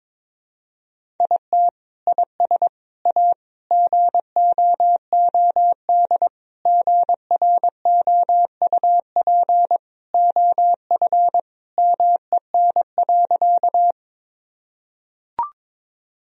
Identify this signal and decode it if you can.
Morse code, heard as a keyed tone, about 21 wpm, 700 Hz. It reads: IT IS A GOOD GROUP OF MEN.
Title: Sentences from Top 1000 Words - Review Part 1 - 22wpm